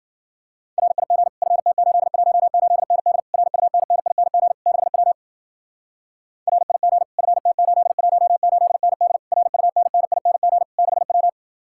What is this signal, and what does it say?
Morse code, heard as a keyed tone, about 48 wpm, 700 Hz. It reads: RIG FT818ND RUNNING 6W RIG FT818ND RUNNING 6W
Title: Speed-Racing Common QSO Phrases - 40wpm